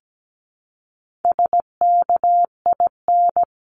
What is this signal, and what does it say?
Morse code, heard as a keyed tone, about 17 wpm, 700 Hz. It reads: SKIN
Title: Top 1001-1250 Words 17wpm